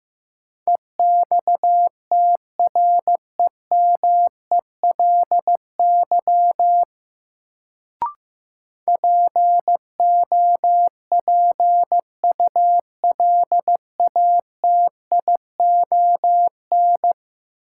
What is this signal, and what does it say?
Morse code, heard as a keyed tone, about 15 wpm, 700 Hz. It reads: EXTREMELY POPULATION
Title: Sentences from Top 1000 Words - Review Part 7 - New Word Drill - 15wpm